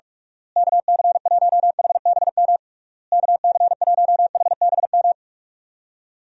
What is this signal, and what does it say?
Morse code, heard as a keyed tone, about 47 wpm, 700 Hz. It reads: KC1HBM KC1HBM
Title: USA Call Signs - 6 Characters - Repeated 3x - 45wpm